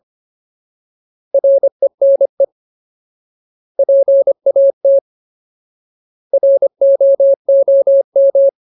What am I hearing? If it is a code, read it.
Morse code, heard as a keyed tone, about 25 wpm, 550 Hz. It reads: RENE PAT ROOM
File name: Ultimate Word List / 25WPM Ultimate